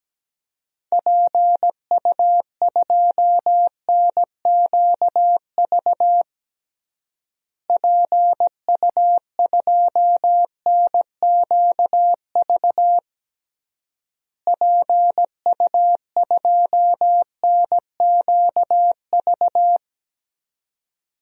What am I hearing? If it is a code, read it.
Morse code, heard as a keyed tone, about 17 wpm, 700 Hz. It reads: PU2NQV PU2NQV PU2NQV
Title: DXCC-100 Call Signs - 6 Characters - Repeated 3x - 17wpm